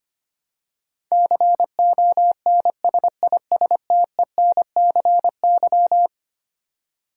Morse code, heard as a keyed tone, about 25 wpm, 700 Hz. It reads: CONSISTENCY
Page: morse-code-ninja-practice-25wpm